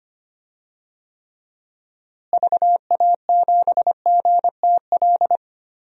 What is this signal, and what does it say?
Morse code, heard as a keyed tone, about 25 wpm, 700 Hz. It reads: VA7GTL
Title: DXCC-100 Call Signs - 6 Characters - Repeated 3x - 25wpm